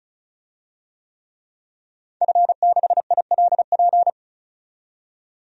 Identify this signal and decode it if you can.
Morse code, heard as a keyed tone, about 34 wpm, 700 Hz. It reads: F6ILP
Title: DXCC-100 Call Signs - 5 Characters - 35wpm